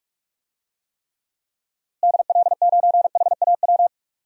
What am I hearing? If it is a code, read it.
Morse code, heard as a keyed tone, about 47 wpm, 700 Hz. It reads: DL9HAW
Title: DXCC-100 Call Signs - 6 Characters - Repeated 3x - 45wpm